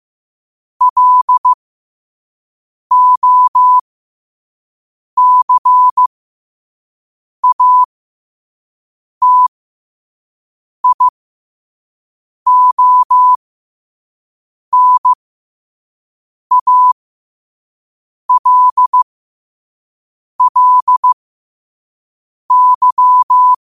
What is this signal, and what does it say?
Morse code, heard as a keyed tone, about 15 wpm, spaced as for 5 wpm, 1000 Hz. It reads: LOCATIONALLY
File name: qotdfiles / 5wpm